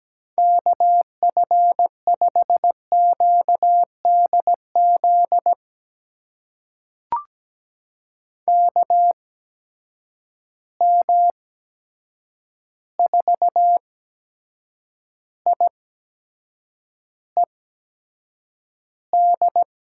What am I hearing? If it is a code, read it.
Morse code, heard as a keyed tone, about 17 wpm, spaced as for 7 wpm, 700 Hz. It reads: KF5QDZ KM4IED
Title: USA Call Signs - 6 Characters - 8x Character Spacing - 17wpm